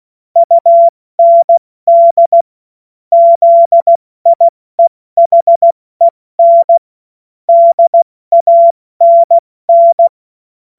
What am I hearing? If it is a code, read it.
Morse code, heard as a keyed tone, about 16 wpm, 680 Hz. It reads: UND ZIEHEN DANN
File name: Witz-23-September-2025-70-F80